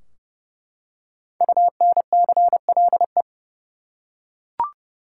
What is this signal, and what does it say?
Morse code, heard as a keyed tone, about 30 wpm, 700 Hz. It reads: UNCLE
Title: Five Letter Words - 30wpm